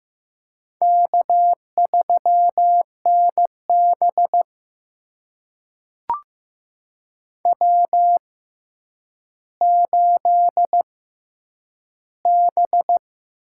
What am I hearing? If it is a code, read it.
Morse code, heard as a keyed tone, about 15 wpm, spaced as for 9 wpm, 700 Hz. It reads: K3NB W8B
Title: USA Call Signs - 4 Characters - 6x Character Spacing - 15wpm